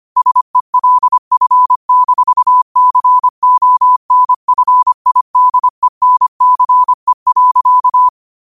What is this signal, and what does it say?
Morse code, heard as a keyed tone, about 25 wpm, 1000 Hz. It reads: SELF-CONFIDENCE.
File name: qotdfiles / 25wpm